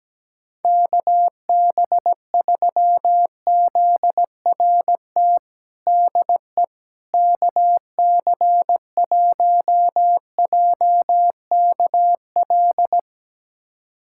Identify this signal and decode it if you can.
Morse code, heard as a keyed tone, about 17 wpm, 700 Hz. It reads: KB3ZRT DE KC1JKL
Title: USA Callsign DE Callsign 17wpm